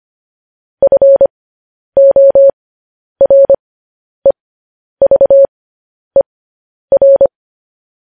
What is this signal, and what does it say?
Morse code, heard as a keyed tone, about 25 wpm, spaced as for 10 wpm, 550 Hz. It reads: FOREVER
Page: news-headlines-in-morse-code-at-10-wpm